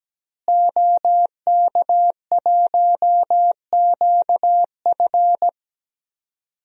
Morse code, heard as a keyed tone, about 17 wpm, 700 Hz. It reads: OK1QF
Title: DXCC-100 Call Signs - 5 Characters - Repeated 2x - 17wpm